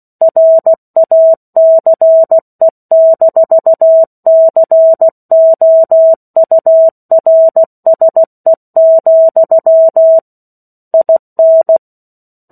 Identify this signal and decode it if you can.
Morse code, heard as a keyed tone, about 16 wpm, 650 Hz. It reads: RACE-COURSE, IN
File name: 03-Chapter-03 16 WPM